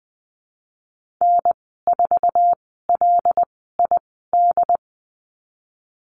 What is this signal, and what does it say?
Morse code, heard as a keyed tone, about 20 wpm, spaced as for 15 wpm, 700 Hz. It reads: N4LID
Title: 088 Lesson 15g - Number 5 - Call Signs 20wpm